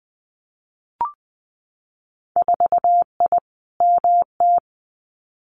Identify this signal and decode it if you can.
Morse code, heard as a keyed tone, about 20 wpm, 700 Hz. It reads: 4I MT
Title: Winter Field Day Exchanges Encore 20wpm